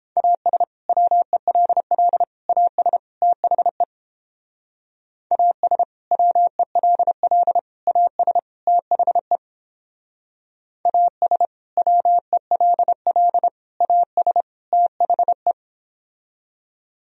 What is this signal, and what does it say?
Morse code, heard as a keyed tone, about 31 wpm, 700 Hz. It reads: AS WELL AS THE AS WELL AS THE AS WELL AS THE